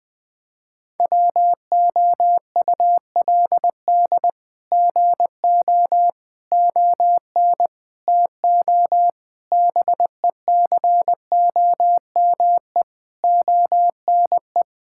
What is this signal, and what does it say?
Morse code, heard as a keyed tone, about 21 wpm, 700 Hz. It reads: WOULD GO ON TO BECOME ONE
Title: Sets of 8 Words - 20wpm